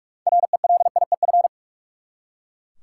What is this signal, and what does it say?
Morse code, heard as a keyed tone, about 43 wpm, 700 Hz. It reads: RELIEF
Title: Six Letter Words - 45wpm